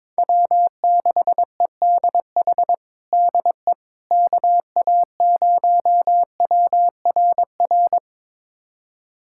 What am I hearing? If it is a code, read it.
Morse code, heard as a keyed tone, about 21 wpm, 700 Hz. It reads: W6EDH DE KA0WRR
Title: USA Callsign DE Callsign 22wpm